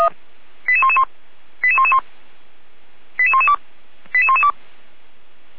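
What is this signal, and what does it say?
Five-tone selective call (ZVEI), heard as quick runs of five tones, tones of about 70 ms. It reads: 88111 88111 88112 88112
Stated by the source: ZVEI five-tone selective call received on SDR.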